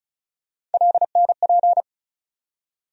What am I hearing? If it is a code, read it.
Morse code, heard as a keyed tone, about 36 wpm, 700 Hz. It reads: LNP
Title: USA Call Sign Postfixes - 3 Letters 35wpm